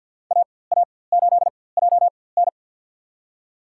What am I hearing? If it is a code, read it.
Morse code, heard as a keyed tone, about 53 wpm, spaced as for 24 wpm, 700 Hz. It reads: AA8JN
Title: USA Call Signs - 5 Characters - 4x Character Spacing - 50wpm